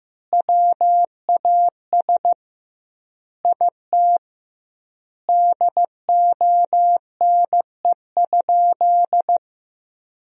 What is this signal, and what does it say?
Morse code, heard as a keyed tone, about 15 wpm, 700 Hz. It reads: WAS IT DONE?
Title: 2x Word-Spacing - Sentences from Top 500 Words - 15wpm